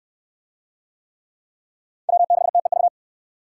Morse code, heard as a keyed tone, about 69 wpm, 700 Hz. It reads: W6TEV